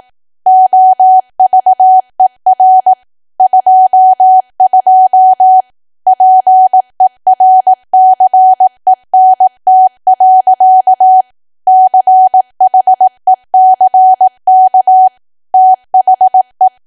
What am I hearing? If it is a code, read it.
Morse code, heard as a keyed tone, about 18 wpm, 750 Hz. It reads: OVER 22 PERCENT. CHECK THE
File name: cwa / W1AW-Level2 / 18WPM